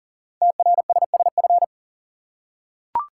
Morse code, heard as a keyed tone, about 40 wpm, 700 Hz. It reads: TRSSF
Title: Code Groups - 5 Letters - 40wpm